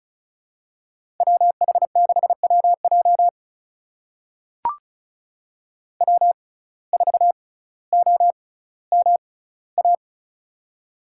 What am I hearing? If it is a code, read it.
Morse code, heard as a keyed tone, about 36 wpm, spaced as for 18 wpm, 700 Hz. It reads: WH6WJ W4OMA
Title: USA Call Signs - 5 Characters - 6x Character Spacing - 35wpm